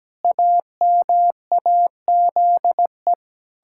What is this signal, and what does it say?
Morse code, heard as a keyed tone, about 17 wpm, 700 Hz. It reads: AMAZE
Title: Pangram Challenge v2 - 17wpm